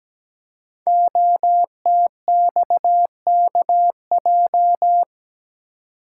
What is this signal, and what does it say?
Morse code, heard as a keyed tone, about 17 wpm, 700 Hz. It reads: OTXKJ
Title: Code Groups - 5 Letter:Numbers - 17wpm